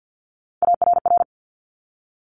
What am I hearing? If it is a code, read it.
Morse code, heard as a keyed tone, about 50 wpm, 700 Hz. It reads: ARR
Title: Top Three Letter Combinations in Words 50wpm